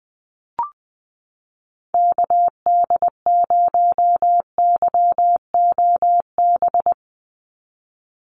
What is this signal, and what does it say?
Morse code, heard as a keyed tone, about 20 wpm, 700 Hz. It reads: KD0YOB
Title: NA SOTA Activator Callsigns 20wpm